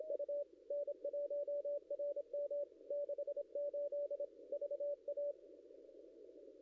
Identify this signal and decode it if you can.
Morse code, heard as a keyed tone, about 26 wpm, 560 Hz. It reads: U N1RM 68 VA
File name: contests / ss-cw-k5tr-op-n5rz -0700